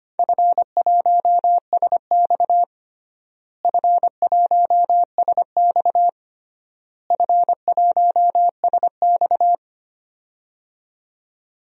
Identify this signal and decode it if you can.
Morse code, heard as a keyed tone, about 25 wpm, 700 Hz. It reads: F1SX F1SX F1SX
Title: DXCC-100 Call Signs - 4 Characters - Repeated 3x - 25wpm